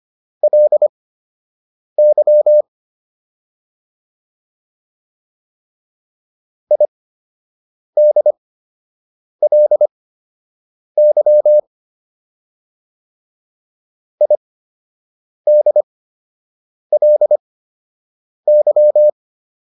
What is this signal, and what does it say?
Morse code, heard as a keyed tone, about 25 wpm, spaced as for 7 wpm, 600 Hz. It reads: LY IDLY IDLY
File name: LY4 07